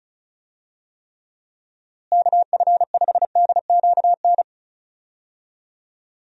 Morse code, heard as a keyed tone, about 34 wpm, 700 Hz. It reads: KF5DQN
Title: USA Call Signs - 6 Characters - Repeated 2x - 35wpm